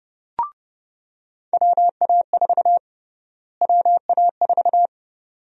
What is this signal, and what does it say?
Morse code, heard as a keyed tone, about 30 wpm, 700 Hz. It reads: WA4 WA4
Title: USA Call Sign Prefixes - Repeated 2x - 30wpm